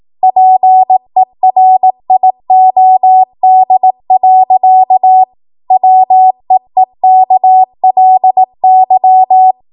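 Morse code, heard as a keyed tone, about 18 wpm, 750 Hz. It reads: PERIOD. WEEKLY
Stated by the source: ARRL 18 wpm Morse code practice archive transmission.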